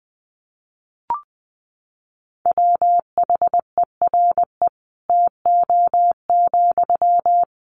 Morse code, heard as a keyed tone, about 20 wpm, 700 Hz. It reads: WHERE TO,
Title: Sentences from Top 200 Words 20wpm